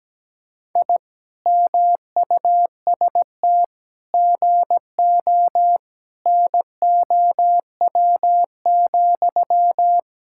Morse code, heard as a keyed tone, about 17 wpm, 700 Hz. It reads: I MUST GO NOW,
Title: Sentences from Top 1000 Words - Review Part 5 - 17wpm